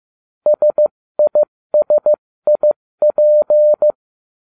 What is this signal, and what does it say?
Morse code, heard as a keyed tone, about 15 wpm, 600 Hz. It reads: SISIP